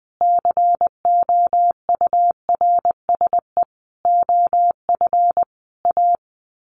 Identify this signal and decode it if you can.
Morse code, heard as a keyed tone, about 20 wpm, 700 Hz. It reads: COURSE OF A
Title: Sets of 5 Words Encore 20wpm